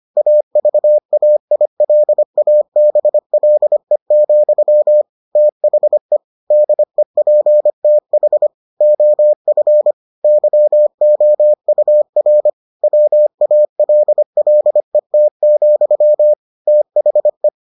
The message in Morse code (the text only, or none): AVAILABLE, THE DEPTH OF YOUR WALLET, THE